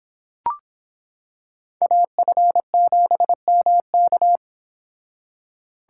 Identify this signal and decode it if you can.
Morse code, heard as a keyed tone, about 25 wpm, 700 Hz. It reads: AF7MK